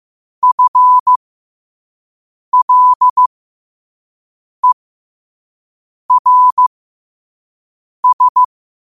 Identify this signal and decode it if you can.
Morse code, heard as a keyed tone, about 15 wpm, spaced as for 5 wpm, 1000 Hz. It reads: FLERS
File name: qotdfiles / 5wpm